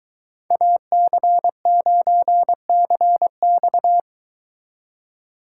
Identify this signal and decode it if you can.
Morse code, heard as a keyed tone, about 23 wpm, 700 Hz. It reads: AC9CX